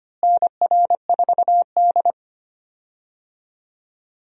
Morse code, heard as a keyed tone, about 25 wpm, 700 Hz. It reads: NR4D
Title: USA Call Signs - 4 Characters - Repeated 2x - 25wpm